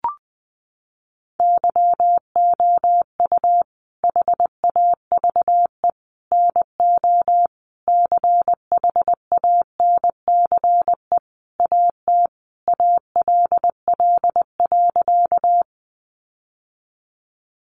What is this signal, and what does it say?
Morse code, heard as a keyed tone, about 20 wpm, 700 Hz. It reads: YOU HAVE NO CHANCE AT ALL.
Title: Sentences from Top 800 Words 20wpm